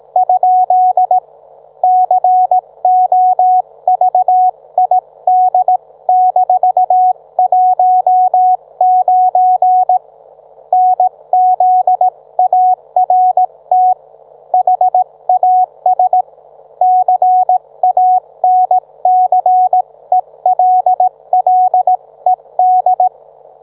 Morse code, heard as a keyed tone, about 18 wpm, 740 Hz. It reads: ? COVID-19 NZART HAS CANCELLED